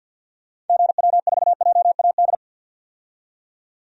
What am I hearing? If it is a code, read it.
Morse code, heard as a keyed tone, about 48 wpm, 700 Hz. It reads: GW4JAD
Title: DXCC-100 Call Signs - 6 Characters - 50wpm